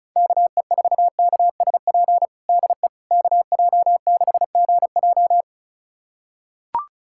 Morse code, heard as a keyed tone, about 34 wpm, 690 Hz. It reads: KE4KSP DE KJ6GJ
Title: USA Callsign DE Callsign 35wpm